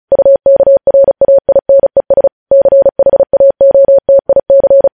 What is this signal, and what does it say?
Morse code, heard as a keyed tone, about 36 wpm, 550 Hz. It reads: UKRAINES CHAOTIC
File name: podcast / News headlines-1x-35x35